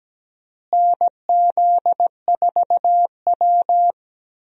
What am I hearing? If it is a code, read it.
Morse code, heard as a keyed tone, about 17 wpm, 700 Hz. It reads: NZ4W
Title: USA Call Signs - 4 Characters - Repeated 3x - 17wpm